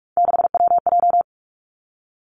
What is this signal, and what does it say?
Morse code, heard as a keyed tone, about 46 wpm, 700 Hz. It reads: 6WJ
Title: Three Challenging Letter-Number Combinations 45wpm